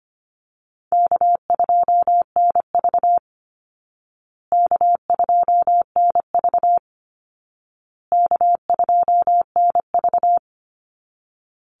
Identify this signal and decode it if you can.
Morse code, heard as a keyed tone, about 25 wpm, 700 Hz. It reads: K2NV K2NV K2NV
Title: Easy Call Signs Repeated 3x - 25wpm